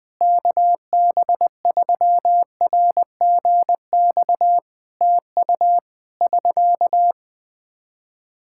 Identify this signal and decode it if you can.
Morse code, heard as a keyed tone, about 20 wpm, 700 Hz. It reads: KB3RGX TU <SK>
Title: Practice QSO - Exchange 4 - Ending - 20wpm